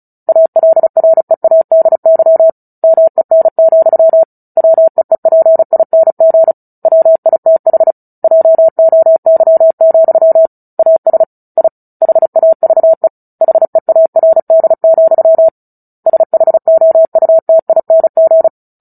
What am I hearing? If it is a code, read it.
Morse code, heard as a keyed tone, about 34 wpm, 650 Hz. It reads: ALREADY MEN, WEEPING WITH JOY, AS I HAVE HEARD, SHOUTING
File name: Book Two / 09-Chapter 9 - 35 WPM